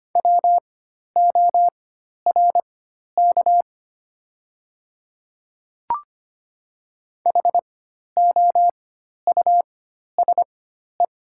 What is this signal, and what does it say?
Morse code, heard as a keyed tone, about 25 wpm, spaced as for 12 wpm, 690 Hz. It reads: WORK HOUSE
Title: 4x Character Spacing - Top 200 Words - 25wpm